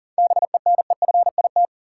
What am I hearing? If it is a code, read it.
Morse code, heard as a keyed tone, about 40 wpm, 700 Hz. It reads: BENEFIT
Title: Sentences from Top 1300 Words - New Word Drill Alt - 40wpm